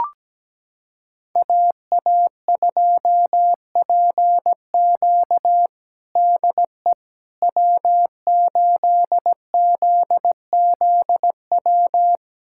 Morse code, heard as a keyed tone, about 17 wpm, 700 Hz. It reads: AA2PQ DE W8ZZW
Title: USA Callsign DE Callsign 17wpm